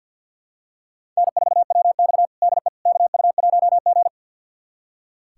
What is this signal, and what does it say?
Morse code, heard as a keyed tone, about 46 wpm, 700 Hz. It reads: N4WX DE KU1G
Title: USA Callsign DE Callsign - 4 Characters - 50wpm